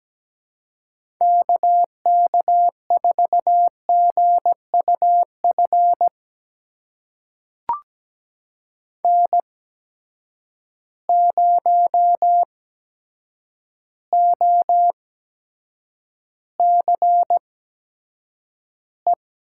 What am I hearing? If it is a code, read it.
Morse code, heard as a keyed tone, about 17 wpm, spaced as for 8 wpm, 700 Hz. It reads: KK4GUF N0OCE
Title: USA Call Signs - 8x Character Spacing - 17wpm